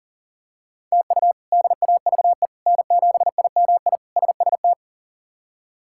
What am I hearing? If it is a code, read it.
Morse code, heard as a keyed tone, about 40 wpm, 690 Hz. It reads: TU DAVE N7IMI SST
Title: K1USN SST QSO Practice - 40wpm